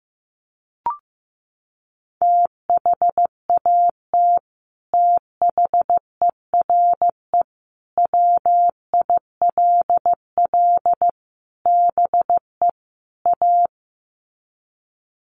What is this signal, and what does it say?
Morse code, heard as a keyed tone, about 15 wpm, 700 Hz. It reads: THAT THERE WILL BE A
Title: Sets of 5 Words Encore2 15wpm